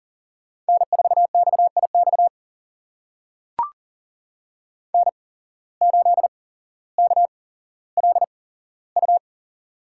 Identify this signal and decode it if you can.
Morse code, heard as a keyed tone, about 40 wpm, spaced as for 17 wpm, 700 Hz. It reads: N4XIX N8KLU
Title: USA Call Signs - 5 Characters - 8x Character Spacing - 40wpm